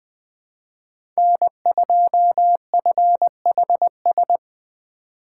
Morse code, heard as a keyed tone, about 20 wpm, 700 Hz. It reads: N2FHS
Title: USA Call Signs - 5 Characters - 6x Character Spacing - 20wpm